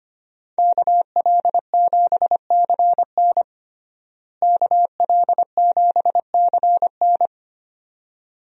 Morse code, heard as a keyed tone, about 25 wpm, 700 Hz. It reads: KL7CN KL7CN
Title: USA Call Signs - 5 Characters - Repeated 2x - 25wpm